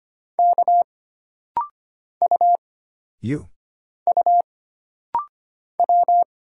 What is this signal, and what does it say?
Morse code, heard as a keyed tone, about 25 wpm, 700 Hz. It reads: K U U W
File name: Single Character (Letter:Number:Punctuation:Prosign) - Rapid-Fire - 25wpm